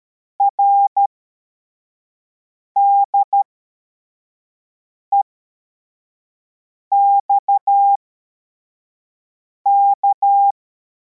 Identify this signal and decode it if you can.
Morse code, heard as a keyed tone, about 13 wpm, spaced as for 4 wpm, 800 Hz. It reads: RDEXK
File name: formation / cw / Text24